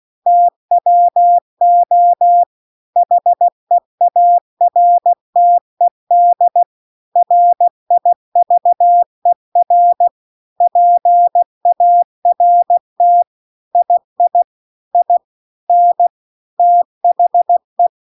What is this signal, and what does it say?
Morse code, heard as a keyed tone, about 16 wpm, 700 Hz. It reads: TWO HEARTED RIVER PART II IN THE
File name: 16WPM